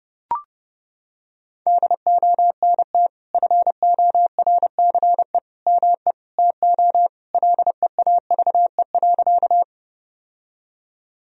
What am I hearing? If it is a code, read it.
Morse code, heard as a keyed tone, about 32 wpm, 700 Hz. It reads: DONT FORCE ME TO LEAVE.
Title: Sentences from Top 1000 Words - Review Part 3 - 30wpm